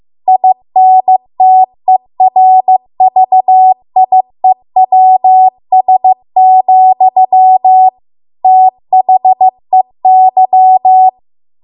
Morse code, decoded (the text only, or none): INTERVIEWS, THEY